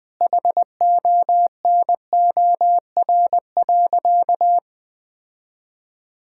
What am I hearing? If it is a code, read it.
Morse code, heard as a keyed tone, about 20 wpm, 700 Hz. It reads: HONOR.
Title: Sentences with 5-Word Phrases - Spoken Beginning - 20wpm